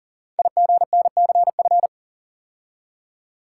Morse code, heard as a keyed tone, about 40 wpm, 700 Hz. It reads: IGNCF
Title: Code Groups - 5 Letters - 40wpm